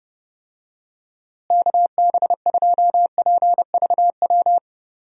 Morse code, heard as a keyed tone, about 32 wpm, 700 Hz. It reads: KB2PVW
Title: USA Call Signs - 6 Characters - Repeated 2x - 30wpm